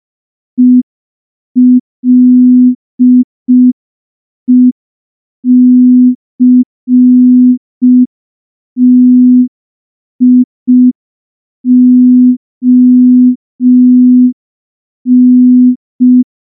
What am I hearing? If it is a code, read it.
Morse code, heard as a keyed tone, about 5 wpm, 250 Hz. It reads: ELECTION
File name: morse / morse-training-05WPM